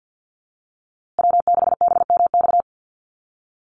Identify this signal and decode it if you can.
Morse code, heard as a keyed tone, about 49 wpm, 700 Hz. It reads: W6BMX
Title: USA Call Signs Repeated 3x 50wpm